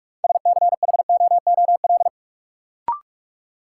Morse code, heard as a keyed tone, about 46 wpm, 700 Hz. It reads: SCHOOL
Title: Sentences from Top 300 Words - New Word Drill - 45wpm